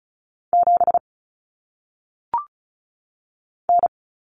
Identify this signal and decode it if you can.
Morse code, heard as a keyed tone, about 36 wpm, 700 Hz. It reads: 7 N
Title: Common Cut Numbers - 1-Digit 35wpm